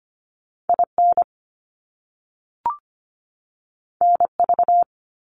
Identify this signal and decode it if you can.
Morse code, heard as a keyed tone, about 25 wpm, 700 Hz. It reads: IN NV
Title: US States 25wpm